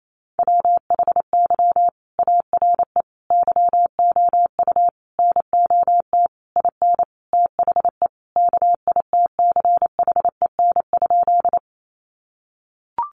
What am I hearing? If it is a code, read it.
Morse code, heard as a keyed tone, about 28 wpm, 700 Hz. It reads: WHY ARE YOU NOT IN THE KITCHEN?
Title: Sentences from Top 1400 Words 28wpm